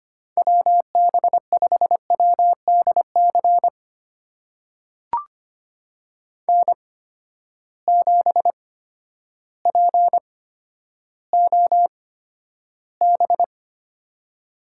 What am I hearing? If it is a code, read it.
Morse code, heard as a keyed tone, about 25 wpm, spaced as for 11 wpm, 700 Hz. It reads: WB5WDC N7POB
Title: USA Call Signs - 8x Character Spacing - 25wpm